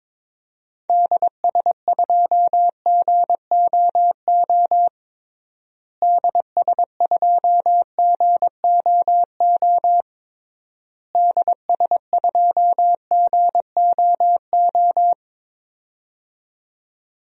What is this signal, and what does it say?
Morse code, heard as a keyed tone, about 22 wpm, 700 Hz. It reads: DS2GOO DS2GOO DS2GOO
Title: DXCC-100 Call Signs - 6 Characters - Repeated 3x - 22wpm